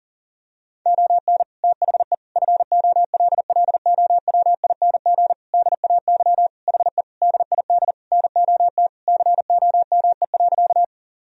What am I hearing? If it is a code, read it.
Morse code, heard as a keyed tone, about 40 wpm, 700 Hz. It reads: ON THE FOLLOWING DAY HE DID NOT COME.
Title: Sentences from Top 1000 Words - Review Part 4 - 40wpm